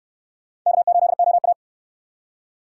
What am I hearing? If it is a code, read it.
Morse code, heard as a keyed tone, about 61 wpm, 700 Hz. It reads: K8OA